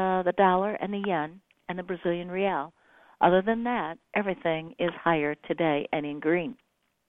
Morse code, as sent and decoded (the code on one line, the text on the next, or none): none